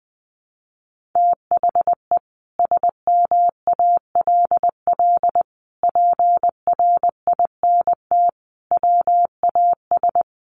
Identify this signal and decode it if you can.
Morse code, heard as a keyed tone, about 20 wpm, 700 Hz. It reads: THE SMALL PRINT WAS
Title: Sentences from Top 1400 Words 20wpm